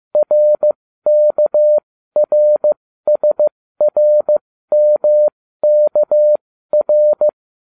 Morse code, heard as a keyed tone, about 15 wpm, spaced as for 13 wpm, 600 Hz. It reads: RKRSRMKR